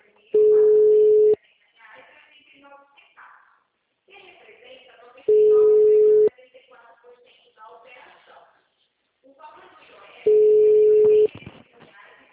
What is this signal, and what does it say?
Ringback tone (Europe, 425 Hz), with speech.